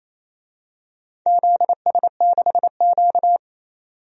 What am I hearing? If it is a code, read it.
Morse code, heard as a keyed tone, about 29 wpm, 690 Hz. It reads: ZS6Q
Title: DXCC-100 Call Signs - 4 Characters - 28wpm